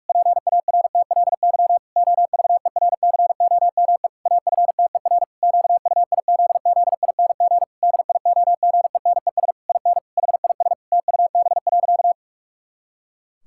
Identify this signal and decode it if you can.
Morse code, heard as a keyed tone, about 46 wpm, 700 Hz. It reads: PARTLY OVERCOME AFTER QUIZZING DIOGENES IN HIS TUB.
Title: Pangram Challenge v2 - 45wpm